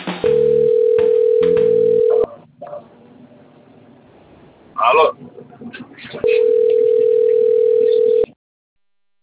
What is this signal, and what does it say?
Ringback tone (North America), 440 + 480 Hz.